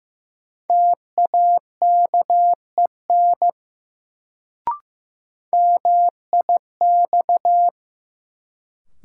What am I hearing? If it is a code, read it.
Morse code, heard as a keyed tone, about 15 wpm, 700 Hz. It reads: TAKEN MIX
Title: Sentences from Top 1000 Words - Review Part 4 - New Word Drill - 15wpm